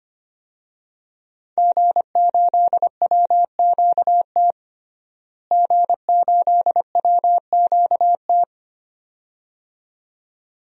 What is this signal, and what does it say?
Morse code, heard as a keyed tone, about 25 wpm, 700 Hz. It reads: G8WQT G8WQT
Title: DXCC-100 Call Signs - 5 Characters - Repeated 2x - 25wpm